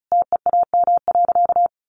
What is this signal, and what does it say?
Morse code, heard as a keyed tone, about 34 wpm, 700 Hz. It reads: TEAM.
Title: Sentences from Top 1500 Words 35wpm